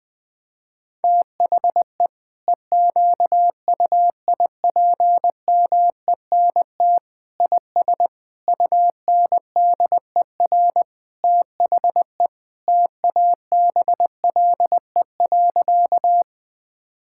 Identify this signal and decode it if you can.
Morse code, heard as a keyed tone, about 21 wpm, 700 Hz. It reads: THE EQUIPMENT IS UNDER THE TABLE.
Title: Sentences from Top 1000 Words - Review Part 5 - 20wpm